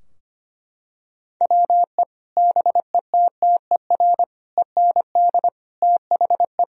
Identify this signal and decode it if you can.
Morse code, heard as a keyed tone, about 25 wpm, 700 Hz. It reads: WE BETTER END THE
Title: Sentences from Top 1000 Words - Review Part 3 - 25wpm